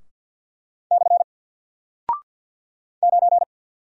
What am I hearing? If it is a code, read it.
Morse code, heard as a keyed tone, about 49 wpm, 700 Hz. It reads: / 9
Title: Single Character (Letter:Number:Punctuation:Prosign) Alt 50wpm